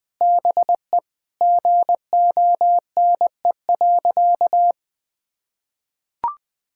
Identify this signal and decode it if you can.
Morse code, heard as a keyed tone, about 21 wpm, 700 Hz. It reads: BE GONE.
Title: Sentences with 4-Word Phrases - Spoken Beginning - 20wpm